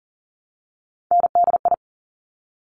Morse code, heard as a keyed tone, about 40 wpm, 700 Hz. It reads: NDI
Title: Top Three Letter Combinations in Words 40wpm